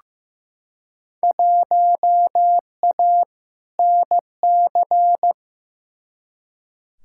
Morse code, heard as a keyed tone, about 15 wpm, 700 Hz. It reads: 1A NC
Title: Field Day Exchanges Encore - 15wpm